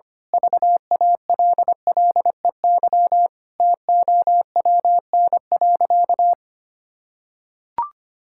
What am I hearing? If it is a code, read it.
Morse code, heard as a keyed tone, about 25 wpm, 700 Hz. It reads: VALLEY TOWN.
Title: Pangram Challenge v2 - 25wpm